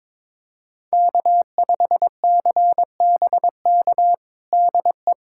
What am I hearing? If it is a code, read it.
Morse code, heard as a keyed tone, about 22 wpm, 700 Hz. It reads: K5CBK DE
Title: USA Callsign DE Callsign 22wpm